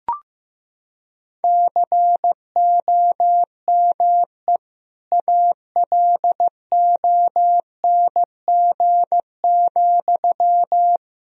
Morse code, heard as a keyed tone, about 15 wpm, 700 Hz. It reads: COME ALONG,
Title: Sentences from Top 1000 Words - Review Part 1 - 15wpm